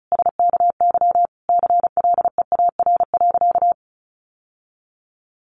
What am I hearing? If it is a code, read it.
Morse code, heard as a keyed tone, about 34 wpm, 700 Hz. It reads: SKY CLEAR.
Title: Sentences from Top 1200 Words 35wpm